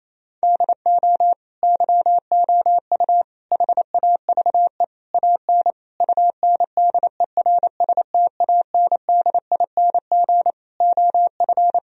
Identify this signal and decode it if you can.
Morse code, heard as a keyed tone, about 28 wpm, 700 Hz. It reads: DO YOU HAVE AN UNDERSTANDING OF
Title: Sentences from Top 1000 Words - Review Part 3 - 28wpm